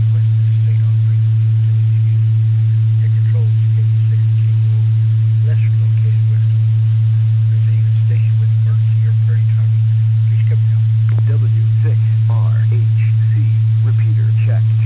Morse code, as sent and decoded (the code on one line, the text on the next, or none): none